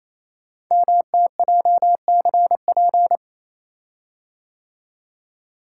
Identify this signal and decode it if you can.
Morse code, heard as a keyed tone, about 27 wpm, 700 Hz. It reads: MTJCP
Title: Code Groups - 5 Letters - 28wpm